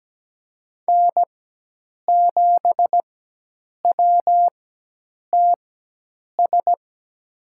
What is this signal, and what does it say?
Morse code, heard as a keyed tone, about 17 wpm, spaced as for 8 wpm, 700 Hz. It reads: N7WTS
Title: USA Call Signs - 4x Character Spacing - 17wpm